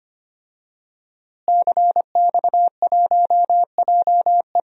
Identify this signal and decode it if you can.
Morse code, heard as a keyed tone, about 25 wpm, 700 Hz. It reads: CX1JE
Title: DXCC-100 Call Signs - 5 Characters - Repeated 3x - 25wpm